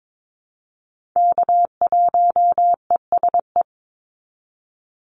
Morse code, heard as a keyed tone, about 22 wpm, 700 Hz. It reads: K1ESE
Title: Call Signs - 22wpm